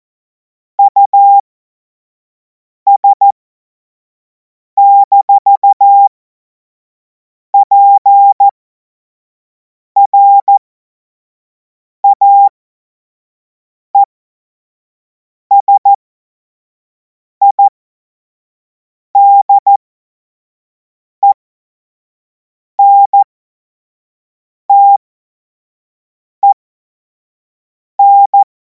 Morse code, heard as a keyed tone, about 14 wpm, spaced as for 5 wpm, 800 Hz. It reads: US-PRAESIDENTEN
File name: Morsekurs / Witze / Witz 25. August 2026 25 F70